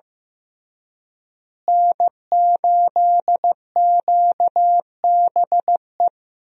Morse code, heard as a keyed tone, about 15 wpm, 700 Hz. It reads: N8QBE
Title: USA Call Signs - 5 Characters - Repeated 2x - 15wpm